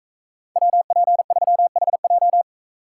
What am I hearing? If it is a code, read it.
Morse code, heard as a keyed tone, about 42 wpm, 700 Hz. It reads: WP3HJ